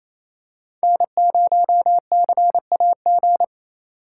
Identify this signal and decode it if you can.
Morse code, heard as a keyed tone, about 28 wpm, 700 Hz. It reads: N0CAG
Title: USA Call Signs - 5 Characters - 2x Character Spacing - 28wpm